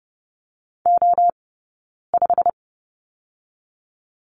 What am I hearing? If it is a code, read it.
Morse code, heard as a keyed tone, about 30 wpm, spaced as for 9 wpm, 700 Hz. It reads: O5
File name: Lesson 20d - Number 3 - Two Characters 30wpm